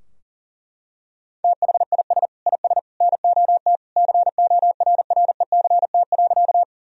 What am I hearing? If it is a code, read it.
Morse code, heard as a keyed tone, about 40 wpm, 700 Hz. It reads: THIS IS NOT CORRECT.
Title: Sentences from Top 1000 Words - Review Part 2 - 40wpm